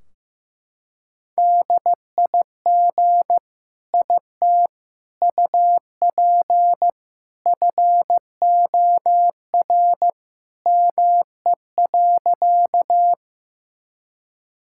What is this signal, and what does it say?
Morse code, heard as a keyed tone, about 15 wpm, 700 Hz. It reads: DIG IT UP FOR ME.
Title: Sentences from Top 1600 Words 15wpm